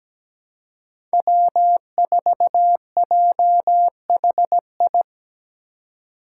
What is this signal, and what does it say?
Morse code, heard as a keyed tone, about 17 wpm, 700 Hz. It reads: W4JHI
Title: Five Challenging Letter-Number Combinations 17wpm